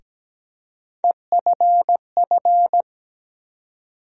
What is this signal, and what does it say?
Morse code, heard as a keyed tone, about 17 wpm, 700 Hz. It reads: EFF
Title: USA Call Sign Postfixes - 3 Letters 17wpm